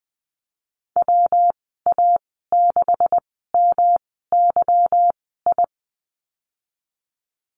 Morse code, heard as a keyed tone, about 20 wpm, spaced as for 15 wpm, 700 Hz. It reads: WA6MYI
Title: 167 Lesson 27g - Number 7 - Call Signs 20wpm